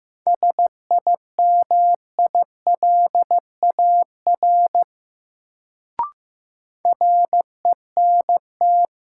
Morse code, heard as a keyed tone, about 15 wpm, 700 Hz. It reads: SIMILAR RENT
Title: Sentences from Top 1000 Words - Review Part 5 - New Word Drill - 15wpm